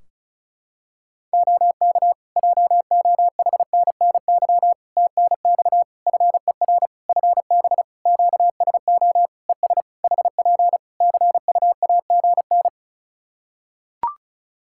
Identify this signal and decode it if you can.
Morse code, heard as a keyed tone, about 36 wpm, 700 Hz. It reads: OK JOHNNY TNX FER FB QSO ES HP CUAGN
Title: Practice QSO - Exchange 4 - Ending - 35wpm